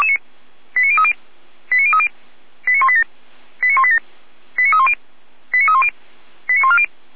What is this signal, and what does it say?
Five-tone selective call (ZVEI), heard as quick runs of five tones, tones of about 70 ms. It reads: 89030 89030 89187 89187 89310 89310 89140